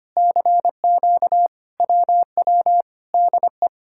Morse code, heard as a keyed tone, about 25 wpm, 700 Hz. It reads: CQ WW DE
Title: Speed-Racing Common QSO Phrases - 25wpm